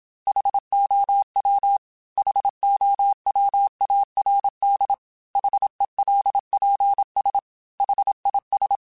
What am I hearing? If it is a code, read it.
Morse code, heard as a keyed tone, about 27 wpm, 800 Hz. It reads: HOW HOWARD HELPS HIS
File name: H Runs / H Runs mp3s / H Runs @25